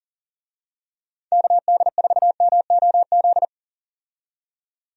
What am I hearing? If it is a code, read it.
Morse code, heard as a keyed tone, about 40 wpm, 700 Hz. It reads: KD4MOZ
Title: USA Call Signs - 6 Characters - 40wpm